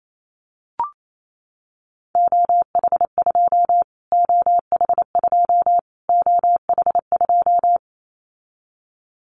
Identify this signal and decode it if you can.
Morse code, heard as a keyed tone, about 28 wpm, 700 Hz. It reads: OH2 OH2 OH2
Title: Callsign Prefixes for Top 25 DXCC - Repeated 3x - 28wpm